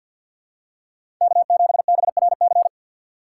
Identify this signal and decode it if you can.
Morse code, heard as a keyed tone, about 48 wpm, 700 Hz. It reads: K7BRC
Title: USA Call Signs - 3x Character Spacing - 50wpm